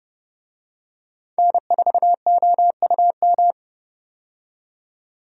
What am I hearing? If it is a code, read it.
Morse code, heard as a keyed tone, about 30 wpm, 700 Hz. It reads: N4OUM
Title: USA Call Signs - 5 Characters - 30wpm